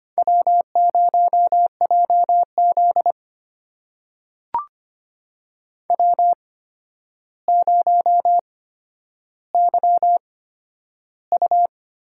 Morse code, heard as a keyed tone, about 25 wpm, spaced as for 11 wpm, 700 Hz. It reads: W0JZ W0YU
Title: USA Call Signs - 4 Characters - 8x Character Spacing - 25wpm